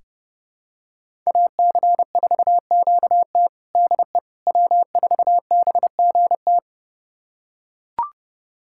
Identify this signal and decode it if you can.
Morse code, heard as a keyed tone, about 30 wpm, 700 Hz. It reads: AC4QT DE W4BGT
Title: USA Callsign DE Callsign - 5 Characters - 30wpm